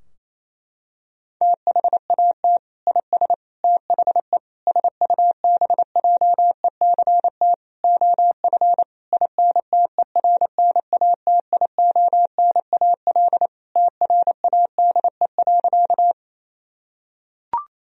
Morse code, heard as a keyed tone, about 28 wpm, 700 Hz. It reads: THAT IS THE SUBJECT OF INTERNATIONAL TRADE.
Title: Sentences from Top 1000 Words - Review Part 4 - 28wpm